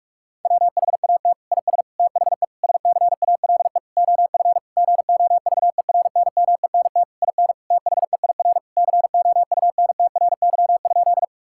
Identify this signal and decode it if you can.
Morse code, heard as a keyed tone, about 46 wpm, 700 Hz. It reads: WHAT IS THE SCALE OF GOVERNMENT IN THEIR COUNTRY?
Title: Sentences from Top 1000 Words - Review Part 5 - 45wpm